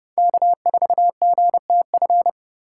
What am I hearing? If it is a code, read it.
Morse code, heard as a keyed tone, about 30 wpm, 700 Hz. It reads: K4GTF
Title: USA Call Signs - 5 Characters - 2x Character Spacing - 30wpm